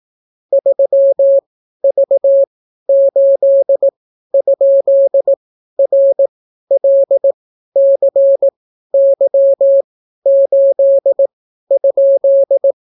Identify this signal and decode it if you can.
Morse code, heard as a keyed tone, about 18 wpm, spaced as for 12 wpm, 550 Hz. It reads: 3V8?RLCY8?